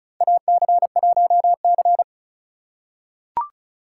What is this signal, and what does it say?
Morse code, heard as a keyed tone, about 36 wpm, 700 Hz. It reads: AC1C
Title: USA Call Signs - 3x Character Spacing - 35wpm